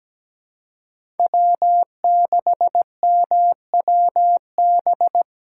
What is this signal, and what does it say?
Morse code, heard as a keyed tone, about 17 wpm, 700 Hz. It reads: W6MWB
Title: USA Call Signs - 5 Characters - 3x Character Spacing - 17wpm